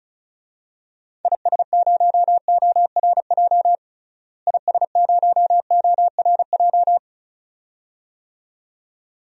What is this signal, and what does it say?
Morse code, heard as a keyed tone, about 36 wpm, 700 Hz. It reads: IS0ORJ IS0ORJ
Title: DXCC-100 Call Signs - 6 Characters - Repeated 2x - 35wpm